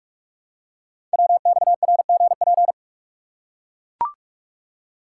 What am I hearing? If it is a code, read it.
Morse code, heard as a keyed tone, about 44 wpm, 700 Hz. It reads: WXRGP
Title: Code Groups - 5 Letters - 45wpm